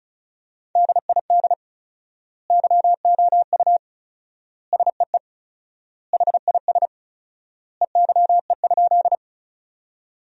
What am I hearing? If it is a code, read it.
Morse code, heard as a keyed tone, about 34 wpm, 700 Hz. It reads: DID YOU SEE HIS EYE?
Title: 4x Word-Spacing - Sentences from Top 300 Words - 35wpm